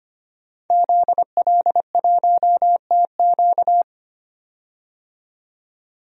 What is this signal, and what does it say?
Morse code, heard as a keyed tone, about 25 wpm, 700 Hz. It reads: ZL1TQ
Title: DXCC-100 Call Signs - 5 Characters - 25wpm